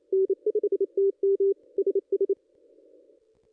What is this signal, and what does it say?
Morse code, heard as a keyed tone, about 27 wpm, 400 Hz. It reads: N5TM SS
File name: ss-cw-k5tr-op-n5rz